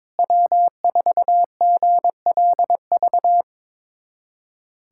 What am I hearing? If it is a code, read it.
Morse code, heard as a keyed tone, about 22 wpm, 700 Hz. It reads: W4GLV
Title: USA Callsign DE Callsign - 5 Characters - 22wpm